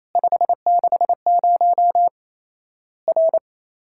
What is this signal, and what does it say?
Morse code, heard as a keyed tone, about 28 wpm, 700 Hz. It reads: 560 R
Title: Worked All Europe DX Contest - QTC Exchange - 28wpm